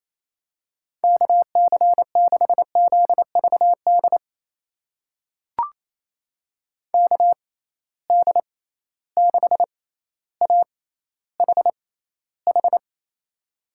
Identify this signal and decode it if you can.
Morse code, heard as a keyed tone, about 27 wpm, spaced as for 14 wpm, 700 Hz. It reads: KC6ZVD KD6AHH
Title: USA Call Signs - 6 Characters - 6x Character Spacing - 28wpm